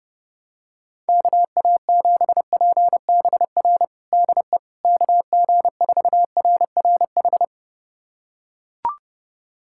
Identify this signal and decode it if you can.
Morse code, heard as a keyed tone, about 30 wpm, 700 Hz. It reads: KA7PBR DE KG4RRH
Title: USA Callsign DE Callsign - 6 Characters - 30wpm